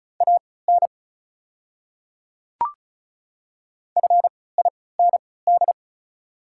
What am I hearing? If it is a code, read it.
Morse code, heard as a keyed tone, about 36 wpm, spaced as for 20 wpm, 700 Hz. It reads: AN FIND
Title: 3x Character Spacing - Top 300 Words - 35wpm